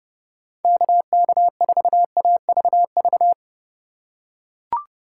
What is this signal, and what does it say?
Morse code, heard as a keyed tone, about 30 wpm, 700 Hz. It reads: KK4AVV